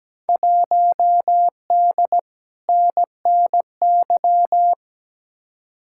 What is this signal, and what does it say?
Morse code, heard as a keyed tone, about 17 wpm, 700 Hz. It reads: 1D NNY
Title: Field Day Exchanges - 17wpm